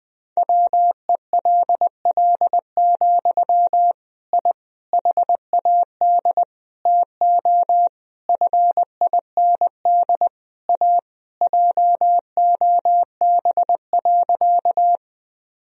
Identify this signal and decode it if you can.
Morse code, heard as a keyed tone, about 20 wpm, 700 Hz. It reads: WELL, I HAD TO FIND A JOB.
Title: Sentences from Top 1000 Words - Review Part 3 - 20wpm